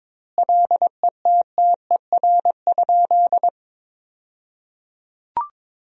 Morse code, heard as a keyed tone, about 22 wpm, 700 Hz. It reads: LETTER?
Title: 2x Word-Spacing - Sentences from Top 300 Words - 22wpm